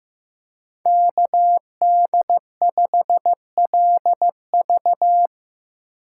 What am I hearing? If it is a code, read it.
Morse code, heard as a keyed tone, about 15 wpm, 700 Hz. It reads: KD5LV
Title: USA Call Signs - 5 Characters - 6x Character Spacing - 15wpm